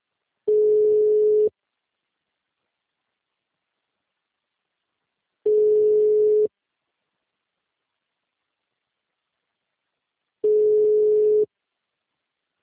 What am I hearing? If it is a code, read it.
Ringback tone (Europe, 425 Hz).
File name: out- -2019- - - .73